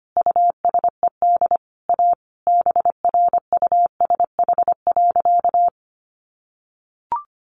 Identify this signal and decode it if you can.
Morse code, heard as a keyed tone, about 25 wpm, 700 Hz. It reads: USED A BRUSH.
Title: Sentences from Top 1100 Words 25wpm